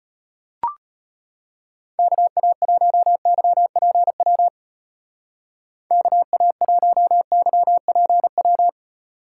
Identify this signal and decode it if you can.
Morse code, heard as a keyed tone, about 32 wpm, 700 Hz. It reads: KA1YPW KA1YPW